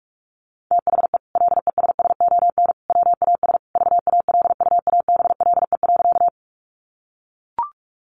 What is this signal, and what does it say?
Morse code, heard as a keyed tone, about 44 wpm, 700 Hz. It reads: THE LESSON WAS VALUABLE.
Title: Sentences from Top 1200 Words 45wpm